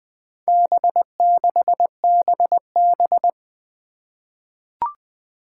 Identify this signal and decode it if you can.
Morse code, heard as a keyed tone, about 20 wpm, 700 Hz. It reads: B6BB
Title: Four Challenging Letter-Number Combinations 20wpm